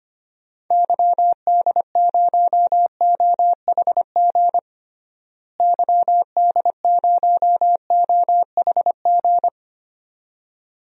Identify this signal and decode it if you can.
Morse code, heard as a keyed tone, about 25 wpm, 700 Hz. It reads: YD0OHG YD0OHG
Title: DXCC-100 Call Signs - 6 Characters - Repeated 2x - 25wpm